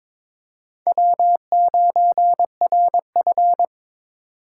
Morse code, heard as a keyed tone, about 22 wpm, 700 Hz. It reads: W9RF
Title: USA Call Signs - 4 Characters - 3x Character Spacing - 22wpm